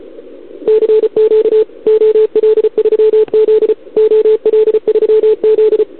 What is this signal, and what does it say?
Morse code, heard as a keyed tone, about 34 wpm, 440 Hz, with radio noise. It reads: CQ OL3Z OL3Z
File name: CQWWCW2021 / OL3Z